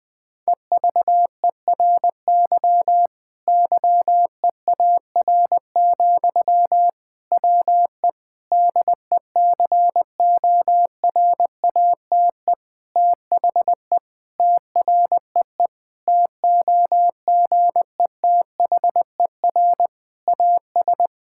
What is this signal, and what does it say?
Morse code, heard as a keyed tone, about 21 wpm, 700 Hz. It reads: EVERY YEAR, WE DECORATE THE TREE TOGETHER AS
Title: Difficult Sentences with Hints - 20wpm